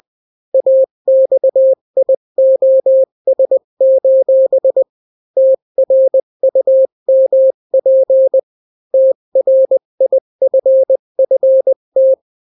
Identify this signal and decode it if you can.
Morse code, heard as a keyed tone, about 20 wpm, 530 Hz. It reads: AXIOS: TRUMP TRIFFT
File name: daily / 18wpm